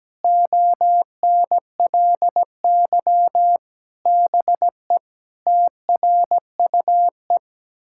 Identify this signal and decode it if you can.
Morse code, heard as a keyed tone, about 17 wpm, 700 Hz. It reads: ONLY BE TRUE
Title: Sentences from Top 1000 Words - Review Part 6 - 17wpm